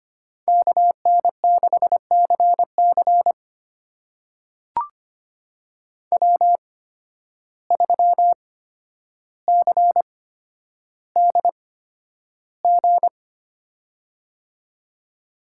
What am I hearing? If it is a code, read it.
Morse code, heard as a keyed tone, about 25 wpm, spaced as for 11 wpm, 700 Hz. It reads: KN6CC W3CDG
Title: USA Call Signs - 5 Characters - 8x Character Spacing - 25wpm